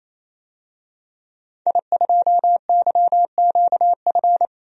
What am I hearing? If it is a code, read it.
Morse code, heard as a keyed tone, about 28 wpm, 700 Hz. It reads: I2YQF
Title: DXCC-100 Call Signs - 5 Characters - Repeated 3x - 28wpm